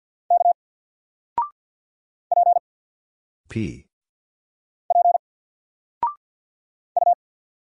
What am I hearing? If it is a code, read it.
Morse code, heard as a keyed tone, about 50 wpm, 700 Hz. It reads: K P P U
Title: Single Character (Letter:Number:Punctuation:Prosign) 50wpm